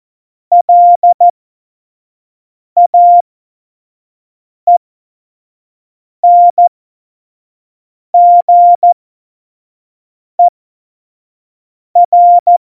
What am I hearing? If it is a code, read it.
Morse code, heard as a keyed tone, about 14 wpm, spaced as for 5 wpm, 700 Hz. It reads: LAENGER
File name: Morsekurs / Witze / Witz 06. November 2018, 25 F70